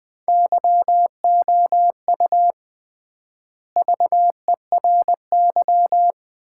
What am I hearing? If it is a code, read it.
Morse code, heard as a keyed tone, about 21 wpm, 700 Hz. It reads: YOU VERY
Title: 3x Word-Spacing - Sets of 3 Words - 20wpm